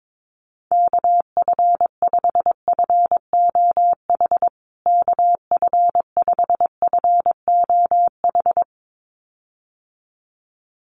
Morse code, heard as a keyed tone, about 22 wpm, 700 Hz. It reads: KF5FOH KF5FOH
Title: USA Call Signs Repeated 2x 22wpm